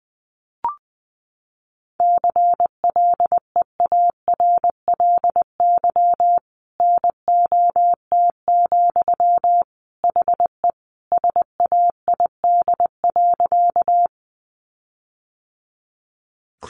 Morse code, heard as a keyed tone, about 20 wpm, 700 Hz. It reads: CLEARLY NOT, HE SAID.
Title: Sentences from Top 1300 Words 20wpm